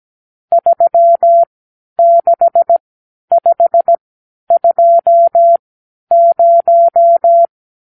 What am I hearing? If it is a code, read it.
Morse code, heard as a keyed tone, about 17 wpm, spaced as for 10 wpm, 680 Hz. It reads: 36520
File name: arrl / Review (1-0)